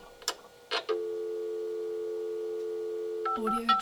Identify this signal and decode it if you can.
Dial tone (North America), 350 + 440 Hz.
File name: preview